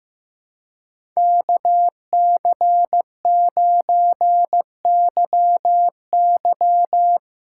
Morse code, heard as a keyed tone, about 16 wpm, 700 Hz. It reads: KC9YY
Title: Speed Racing - USA 2x2 Call Signs 15wpm